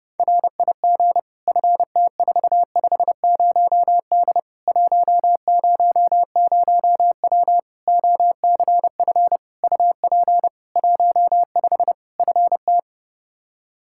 Morse code, heard as a keyed tone, about 30 wpm, 700 Hz. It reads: RIG FT450D 100W OCF UP 15 FT
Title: Practice QSO - Exchange 2 - Info - 30wpm